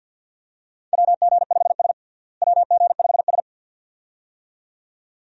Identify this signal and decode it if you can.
Morse code, heard as a keyed tone, about 46 wpm, 700 Hz. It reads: WG5S WG5S
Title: USA Call Signs - 4 Characters - Repeated 2x - 50wpm